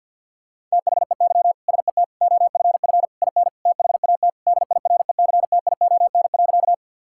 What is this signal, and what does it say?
Morse code, heard as a keyed tone, about 49 wpm, 700 Hz. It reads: THEY SET OFF IN THAT DIRECTION.
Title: Sentences from Top 1000 Words - Review Part 5 - 50wpm